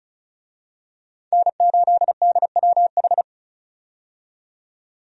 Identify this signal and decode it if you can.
Morse code, heard as a keyed tone, about 36 wpm, 700 Hz. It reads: N8DWH
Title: USA Call Signs - 5 Characters - 35wpm